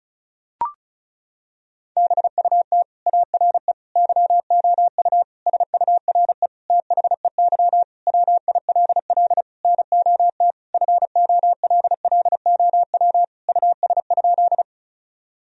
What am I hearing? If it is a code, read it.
Morse code, heard as a keyed tone, about 36 wpm, 700 Hz. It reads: BUT ARE YOU SURE THEY WILL NOT FOLLOW US?
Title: Sentences from Top 1000 Words - Review Part 1 - 35wpm